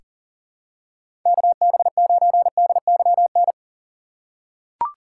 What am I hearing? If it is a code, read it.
Morse code, heard as a keyed tone, about 40 wpm, 700 Hz. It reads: KB9DYN
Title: USA Call Signs - 6 Characters - 2x Character Spacing - 40wpm